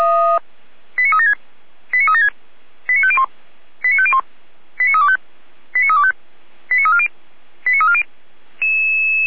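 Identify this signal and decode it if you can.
Five-tone selective call (ZVEI), heard as quick runs of five tones, tones of about 70 ms. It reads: 89376 89376 89501 89501 89325 89325 89340 89340
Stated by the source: ZVEI five-tone selective call received on an SDR, fire brigade Feuerwehr Orlamünde.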